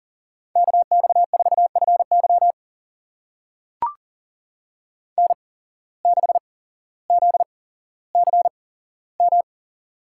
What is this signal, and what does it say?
Morse code, heard as a keyed tone, about 40 wpm, spaced as for 17 wpm, 700 Hz. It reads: KX4FY N6ZCM
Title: USA Call Signs - 5 Characters - 8x Character Spacing - 40wpm